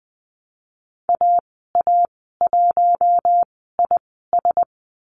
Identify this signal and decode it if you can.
Morse code, heard as a keyed tone, about 20 wpm, spaced as for 15 wpm, 700 Hz. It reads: AA1IS
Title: 042 Lesson 8g - Number 1 - Call Signs 20wpm